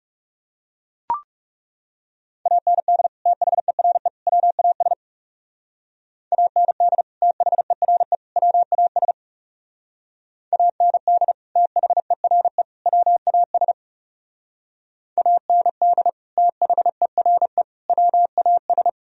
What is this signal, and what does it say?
Morse code, heard as a keyed tone, about 37 wpm, 700 Hz. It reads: AND THERE WAS AND THERE WAS AND THERE WAS AND THERE WAS
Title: Speed-Racing - Sets of 3 Words 30wpm